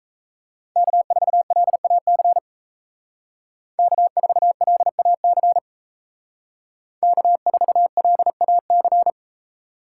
Morse code, heard as a keyed tone, about 36 wpm, 700 Hz. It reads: K4LAC K4LAC K4LAC